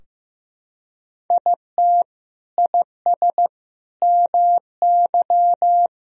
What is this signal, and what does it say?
Morse code, heard as a keyed tone, about 15 wpm, 700 Hz. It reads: IT IS MY
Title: Sentences from Top 1000 Words - Review Part 2 - 15wpm